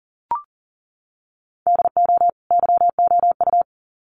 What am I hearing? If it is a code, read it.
Morse code, heard as a keyed tone, about 40 wpm, 700 Hz. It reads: DO YOU
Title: Sets of 2 Words 40wpm